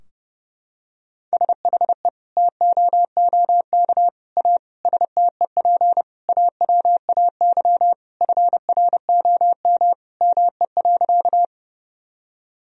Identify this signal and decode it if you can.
Morse code, heard as a keyed tone, about 30 wpm, 700 Hz. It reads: SHE TOOK A STEP AWAY FROM ME.
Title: Sentences from Top 1000 Words - Review Part 2 - 30wpm